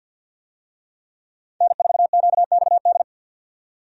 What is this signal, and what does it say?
Morse code, heard as a keyed tone, about 49 wpm, 700 Hz. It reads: N4QXD
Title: USA Call Signs - 5 Characters - 50wpm